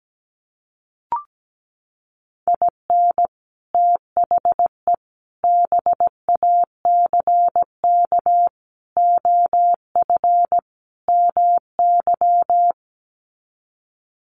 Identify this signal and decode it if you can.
Morse code, heard as a keyed tone, about 17 wpm, 700 Hz. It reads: IN THE BACK OF MY
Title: Sets of 5 Words Encore 17wpm